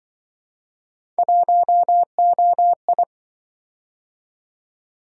Morse code, heard as a keyed tone, about 26 wpm, 700 Hz. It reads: 1OI